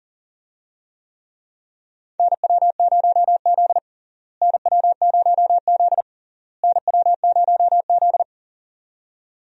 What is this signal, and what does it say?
Morse code, heard as a keyed tone, about 40 wpm, 700 Hz. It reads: NW0Z NW0Z NW0Z
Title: USA Call Signs - 4 Characters - Repeated 3x - 40wpm